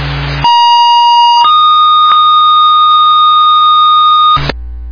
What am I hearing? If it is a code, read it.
Two-tone page, heard as a one-second tone, then a three-second tone, about 950 then 1230 Hz. It is heard from 0.4 to 4.4 s.